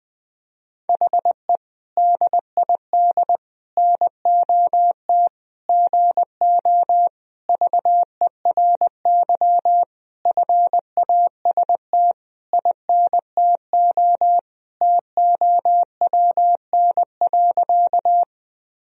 Morse code, heard as a keyed tone, about 20 wpm, 700 Hz. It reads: HE DID NOT GO VERY FAST INTO TOWN.
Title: Sentences from Top 1000 Words - Review Part 2 - 20wpm